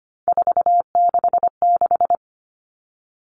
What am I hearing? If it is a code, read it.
Morse code, heard as a keyed tone, about 25 wpm, 700 Hz. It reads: 466
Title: All Cut Numbers - 3-Digit 25wpm